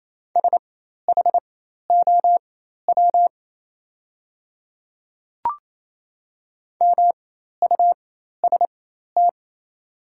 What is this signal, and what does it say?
Morse code, heard as a keyed tone, about 28 wpm, spaced as for 13 wpm, 700 Hz. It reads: SHOW MUST
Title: 4x Character Spacing - Top 300 Words - 28wpm